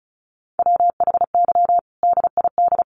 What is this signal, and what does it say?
Morse code, heard as a keyed tone, about 34 wpm, 700 Hz. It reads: WHY DID
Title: Sentences from Top 1000 Words 35wpm